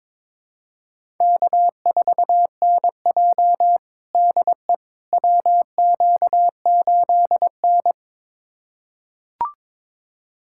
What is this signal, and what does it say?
Morse code, heard as a keyed tone, about 22 wpm, 700 Hz. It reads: K4NJ DE WQ8N
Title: USA Callsign DE Callsign - 4 Characters - 22wpm